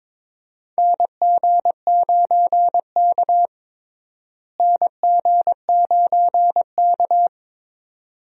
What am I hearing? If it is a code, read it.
Morse code, heard as a keyed tone, about 21 wpm, 700 Hz. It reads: NG9K NG9K
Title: USA Call Signs - 4 Characters - Repeated 2x - 22wpm